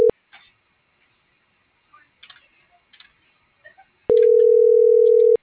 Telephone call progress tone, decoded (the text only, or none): ringback tone